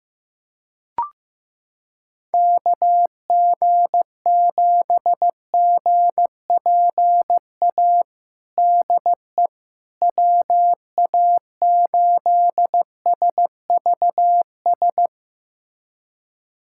Morse code, heard as a keyed tone, about 15 wpm, 700 Hz. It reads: KG7GPA DE WA8SVS
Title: USA Callsign DE Callsign 15wpm